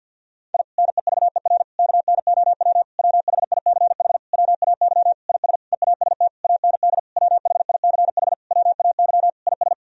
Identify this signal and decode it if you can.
Morse code, heard as a keyed tone, about 52 wpm, 700 Hz. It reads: I NEVER KNOW WHICH WAY IS EAST AND WHICH WAY IS
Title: Sentences from Top 1000 Words - Review Part 3 - 50wpm